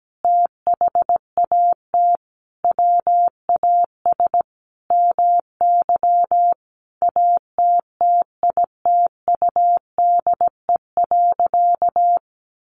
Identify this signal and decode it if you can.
Morse code, heard as a keyed tone, about 17 wpm, 700 Hz. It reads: THAT WAS MY ATTITUDE.
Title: Sentences from Top 1300 Words 17wpm